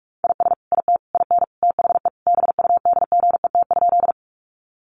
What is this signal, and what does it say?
Morse code, heard as a keyed tone, about 44 wpm, 700 Hz. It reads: IS IT IN THE BUDGET?
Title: Sentences from Top 1200 Words 45wpm